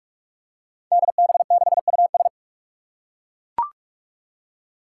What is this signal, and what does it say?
Morse code, heard as a keyed tone, about 42 wpm, 700 Hz. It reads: DB6US
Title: Five Challenging Letter-Number Combinations 45wpm